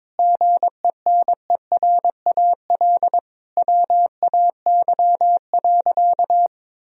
Morse code, heard as a keyed tone, about 22 wpm, 700 Hz. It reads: GENERAL WAY.
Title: Sentences from Top 1000 Words - Review Part 3 - 22wpm